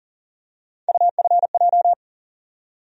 Morse code, heard as a keyed tone, about 40 wpm, 700 Hz. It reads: UFJ
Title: USA Call Sign Postfixes - 3 Letters 40wpm